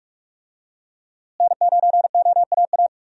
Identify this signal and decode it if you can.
Morse code, heard as a keyed tone, about 41 wpm, 700 Hz. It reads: N9OAA
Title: USA Call Signs - 5 Characters - Repeated 2x - 45wpm